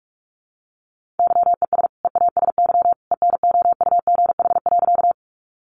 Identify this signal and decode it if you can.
Morse code, heard as a keyed tone, about 44 wpm, 700 Hz. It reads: YES EASY ENOUGH.
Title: Sentences from Top 700 Words 45wpm